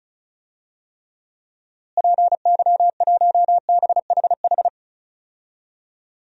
Morse code, heard as a keyed tone, about 36 wpm, 700 Hz. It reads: PY1BHH
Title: DXCC-100 Call Signs - 6 Characters - Repeated 2x - 35wpm